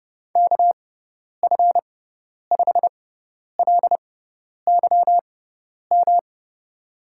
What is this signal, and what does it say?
Morse code, heard as a keyed tone, about 30 wpm, spaced as for 10 wpm, 700 Hz. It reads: KF5LYM
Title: USA Call Signs - 6 Characters - 6x Character Spacing - 30wpm